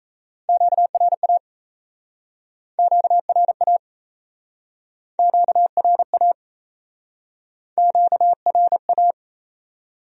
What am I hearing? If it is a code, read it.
Morse code, heard as a keyed tone, about 37 wpm, 700 Hz. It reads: QRA QRA QRA QRA